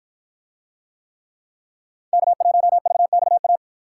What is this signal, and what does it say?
Morse code, heard as a keyed tone, about 53 wpm, 700 Hz. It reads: K1VXA